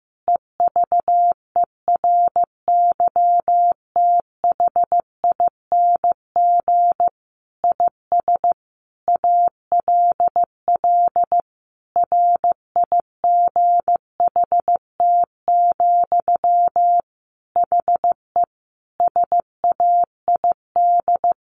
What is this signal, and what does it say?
Morse code, heard as a keyed tone, about 15 wpm, 700 Hz. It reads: EVERYTHING IS ALL RIGHT, HE SAID
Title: Sentences from Top 1100 Words 15wpm